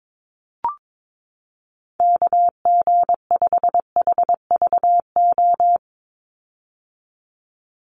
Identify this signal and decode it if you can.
Morse code, heard as a keyed tone, about 22 wpm, 700 Hz. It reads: KG5HVO
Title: Call Signs - 22wpm